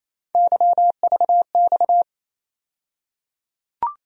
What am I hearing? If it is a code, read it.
Morse code, heard as a keyed tone, about 28 wpm, 700 Hz. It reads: YVX
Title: USA Call Sign Postfixes - 3 Letters 28wpm